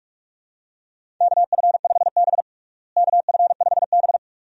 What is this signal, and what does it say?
Morse code, heard as a keyed tone, about 45 wpm, 700 Hz. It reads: KF5B KF5B
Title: USA Call Signs - 4 Characters - Repeated 2x - 45wpm